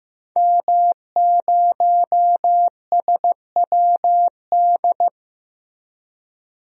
Morse code, heard as a keyed tone, about 15 wpm, 700 Hz. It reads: M0SWD
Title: DXCC-100 Call Signs - 5 Characters - Repeated 2x - 15wpm